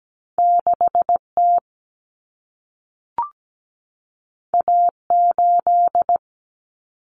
Morse code, heard as a keyed tone, about 17 wpm, 700 Hz. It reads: 6T A8
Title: Common Cut Numbers - 2-Digit 17wpm